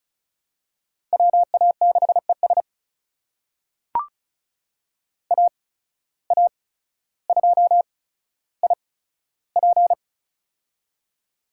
Morse code, heard as a keyed tone, about 36 wpm, spaced as for 15 wpm, 700 Hz. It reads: WA6ES AA2IP
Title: USA Call Signs - 5 Characters - 8x Character Spacing - 35wpm